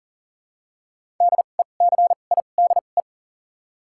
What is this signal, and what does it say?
Morse code, heard as a keyed tone, about 40 wpm, 700 Hz. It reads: D E C I D E
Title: 2x Character Spacing - Top 500 Words - 40wpm